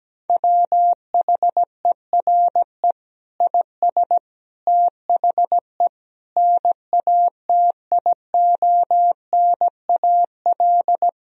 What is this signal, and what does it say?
Morse code, heard as a keyed tone, about 17 wpm, 700 Hz. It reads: WHERE IS THE NATIONAL
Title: Sentences from Top 1000 Words - Review Part 4 - 17wpm